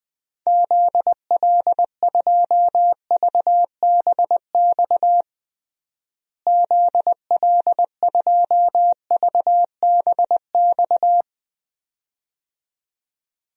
Morse code, heard as a keyed tone, about 20 wpm, 700 Hz. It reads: ZL2VBX ZL2VBX
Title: DXCC-100 Call Signs - 6 Characters - Repeated 2x - 20wpm